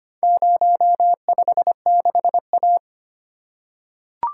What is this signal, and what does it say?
Morse code, heard as a keyed tone, about 25 wpm, 700 Hz. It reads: 056A
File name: practice / 25wpm / Four Letter-Number Combinations - 25wpm